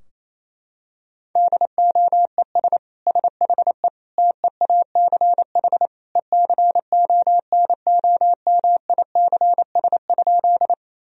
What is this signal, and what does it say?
Morse code, heard as a keyed tone, about 28 wpm, 700 Hz. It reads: DOES SHE TEACH ECONOMICS?
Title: Sentences from Top 1000 Words - Review Part 3 - 28wpm